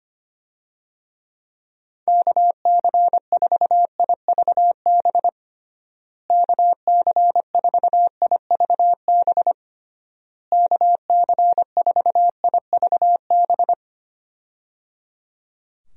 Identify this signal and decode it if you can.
Morse code, heard as a keyed tone, about 25 wpm, 700 Hz. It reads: KC4IVB KC4IVB KC4IVB
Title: USA Call Signs - 6 Characters - Repeated 3x - 25wpm